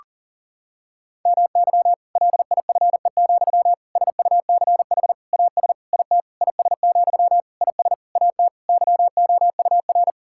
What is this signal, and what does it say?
Morse code, heard as a keyed tone, about 40 wpm, 700 Hz. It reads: MY LIFE, SUCH AS IT IS, IS AT YOUR
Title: Sentences from Top 1600 Words 40wpm